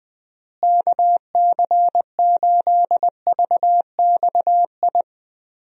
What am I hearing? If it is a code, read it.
Morse code, heard as a keyed tone, about 20 wpm, 700 Hz. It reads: KC8VXI
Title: USA Call Signs - 6 Characters - Repeated 3x - 20wpm